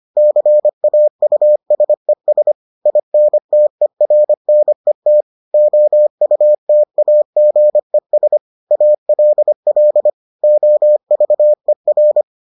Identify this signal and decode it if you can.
Morse code, heard as a keyed tone, about 25 wpm, 600 Hz. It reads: CAUSES INTERNET OUTAGES ALL OVER